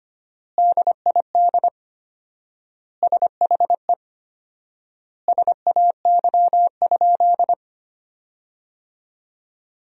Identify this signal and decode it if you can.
Morse code, heard as a keyed tone, about 25 wpm, 700 Hz. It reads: DID SHE SAY?
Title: 4x Word-Spacing - Sentences from Top 500 Words - 25wpm